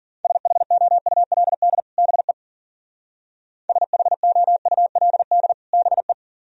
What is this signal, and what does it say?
Morse code, heard as a keyed tone, about 52 wpm, 700 Hz. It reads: SHOULD BE SHOULD BE